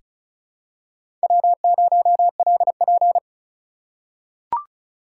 Morse code, heard as a keyed tone, about 34 wpm, 700 Hz. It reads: W0LP
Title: USA Call Signs - 4 Characters - 6x Character Spacing - 35wpm